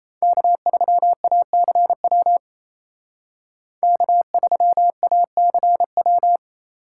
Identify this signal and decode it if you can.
Morse code, heard as a keyed tone, about 36 wpm, 700 Hz. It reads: K3ACW K3ACW